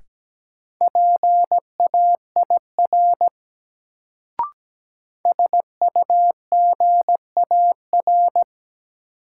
Morse code, heard as a keyed tone, about 17 wpm, 700 Hz. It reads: PAIR SUGAR
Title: Sentences from Top 1200 Words - New Word Drill - 17wpm